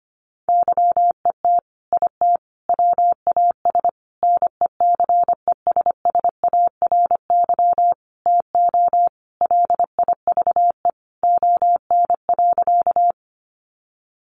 Morse code, heard as a keyed tone, about 25 wpm, 700 Hz. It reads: YET IT WAS NECESSARY TO LIVE ON.
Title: Sentences from Top 800 Words 25wpm